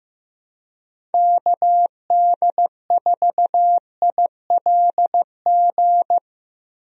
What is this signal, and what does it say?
Morse code, heard as a keyed tone, about 15 wpm, 700 Hz. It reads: KD4ILG
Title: USA Call Signs - 6 Characters - Repeated 2x - 15wpm